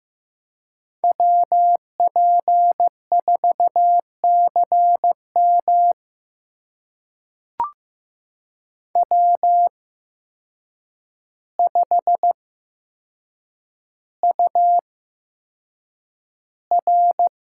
Morse code, heard as a keyed tone, about 15 wpm, spaced as for 7 wpm, 700 Hz. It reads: WP4CM W5UR
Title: USA Call Signs - 8x Character Spacing - 15wpm